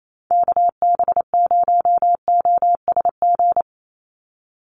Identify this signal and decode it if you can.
Morse code, heard as a keyed tone, about 28 wpm, 700 Hz. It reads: KB0OSG
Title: USA Call Signs 28wpm